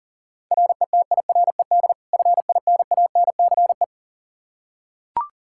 Morse code, heard as a keyed tone, about 40 wpm, 700 Hz. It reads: RETIRED FINANCE
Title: K1USN SST QSO Practice 40wpm